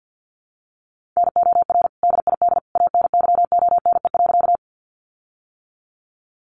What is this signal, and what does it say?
Morse code, heard as a keyed tone, about 50 wpm, 700 Hz. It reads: NOR DID ANYONE.
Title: Sentences from Top 1500 Words 50wpm